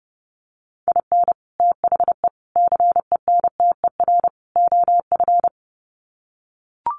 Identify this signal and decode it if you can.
Morse code, heard as a keyed tone, about 30 wpm, 700 Hz. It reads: IN THE CENTER OF
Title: Sets of 4 Words 30wpm